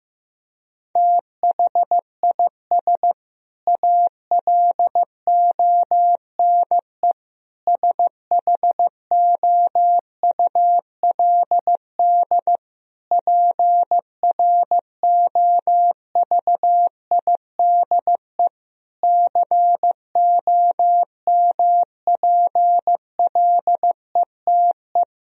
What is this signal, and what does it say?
Morse code, heard as a keyed tone, about 16 wpm, 700 Hz. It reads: THIS ALONE SHOULD PROVIDE COMPLETE
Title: Sentences from Top 1000 Words - Review Part 4 - 15wpm